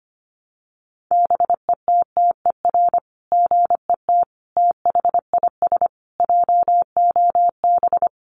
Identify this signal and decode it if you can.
Morse code, heard as a keyed tone, about 25 wpm, 700 Hz. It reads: BETTER GET THIS JOB
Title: Sentences from Top 600 Words 25wpm